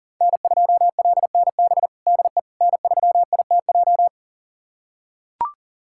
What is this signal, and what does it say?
Morse code, heard as a keyed tone, about 40 wpm, 700 Hz. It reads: N2LNB DE N3ITJ
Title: USA Callsign DE Callsign - 5 Characters - 40wpm